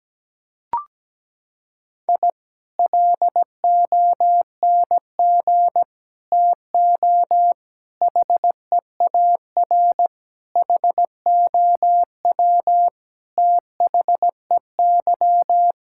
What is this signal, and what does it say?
Morse code, heard as a keyed tone, about 17 wpm, 700 Hz. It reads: I LONG TO HEAR HOW THEY
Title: Sentences from Top 1000 Words - Review Part 5 - 17wpm